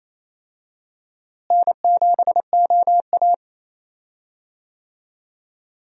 Morse code, heard as a keyed tone, about 28 wpm, 700 Hz. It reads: N7OA
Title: USA Call Signs - 4 Characters - Repeated 3x - 28wpm